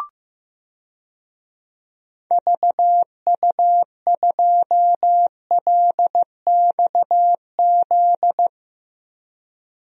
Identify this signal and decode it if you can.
Morse code, heard as a keyed tone, about 15 wpm, 700 Hz. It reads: VU2LXZ
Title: DXCC-100 Call Signs - 6 Characters - 15wpm